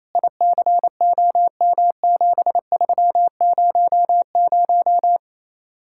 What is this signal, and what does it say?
Morse code, heard as a keyed tone, about 35 wpm, 700 Hz. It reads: ICOM7300
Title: Speed-Racing Common QSO Phrases - 28wpm